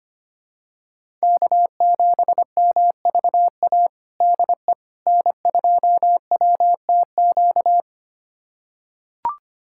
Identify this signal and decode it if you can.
Morse code, heard as a keyed tone, about 25 wpm, 700 Hz. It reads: K7MVA DE N2WTQ
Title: USA Callsign DE Callsign - 5 Characters - 25wpm